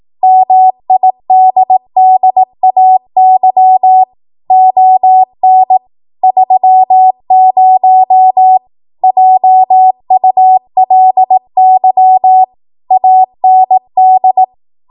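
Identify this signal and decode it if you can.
Morse code, heard as a keyed tone, about 18 wpm, 750 Hz. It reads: MIDDAY ON 30 JULY AND
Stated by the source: 18 WPM Morse code practice from a national amateur radio association's code archive.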